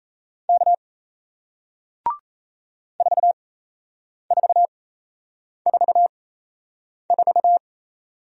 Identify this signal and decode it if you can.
Morse code, heard as a keyed tone, about 38 wpm, spaced as for 8 wpm, 700 Hz. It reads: K 4444